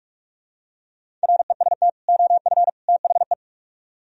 Morse code, heard as a keyed tone, about 46 wpm, 700 Hz. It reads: REST OF THE